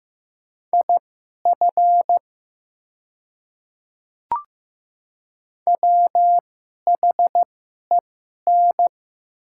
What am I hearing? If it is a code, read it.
Morse code, heard as a keyed tone, about 15 wpm, spaced as for 11 wpm, 700 Hz. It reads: IF WHEN
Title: 2x Character Spacing - Top 100 Words - 15wpm